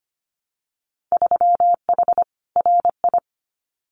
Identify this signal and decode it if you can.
Morse code, heard as a keyed tone, about 25 wpm, 700 Hz. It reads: 3H RI
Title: Winter Field Day Exchanges Encore 25wpm